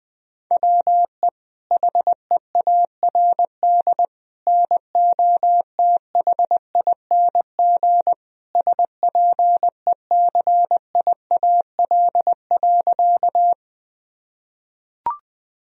Morse code, heard as a keyed tone, about 20 wpm, 700 Hz. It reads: WE HEARD NOTHING SPECIAL.
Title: Sentences from Top 1000 Words - Review Part 3 - 20wpm